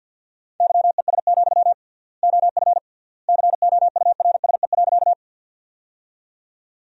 Morse code, heard as a keyed tone, about 48 wpm, 700 Hz. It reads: YES, OF COURSE.
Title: 3x Word-Spacing - Sentences from Top 500 Words - 50wpm